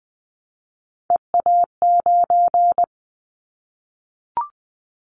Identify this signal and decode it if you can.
Morse code, heard as a keyed tone, about 20 wpm, 700 Hz. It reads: EA9
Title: Callsign Prefixes for Top 100 DXCC - 20wpm